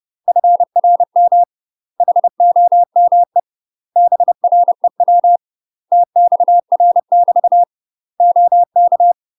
Morse code, heard as a keyed tone, about 30 wpm, 700 Hz. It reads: FRM HOME BREW TXR <BT> OK
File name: UK MASTER QSOs TRAINING / 30WPM / UK MASTER QSOs 0013